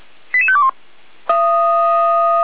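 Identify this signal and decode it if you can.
Five-tone selective call (ZVEI), heard as quick runs of five tones, tones of about 70 ms. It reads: 88521
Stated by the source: ZVEI five-tone selective call received by SDR.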